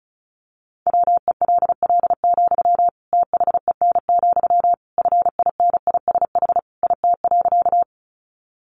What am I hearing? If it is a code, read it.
Morse code, heard as a keyed tone, about 34 wpm, 700 Hz. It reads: WELL, THEN, FINISH IT.
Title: Sentences from Top 1200 Words 35wpm